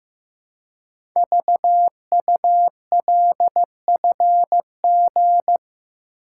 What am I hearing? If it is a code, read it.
Morse code, heard as a keyed tone, about 15 wpm, 700 Hz. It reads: VULFG
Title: Code Groups - 5 Letters - 15wpm